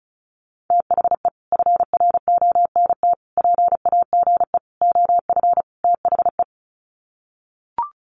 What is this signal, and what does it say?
Morse code, heard as a keyed tone, about 34 wpm, 700 Hz. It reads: THE FRONT PAGE OF THE
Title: Sets of 5 Words Encore 35wpm